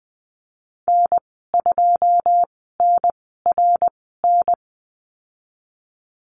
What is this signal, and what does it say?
Morse code, heard as a keyed tone, about 20 wpm, spaced as for 15 wpm, 700 Hz. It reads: N2NRN
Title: 082 Lesson 14g - Number 2 - Call Signs 20wpm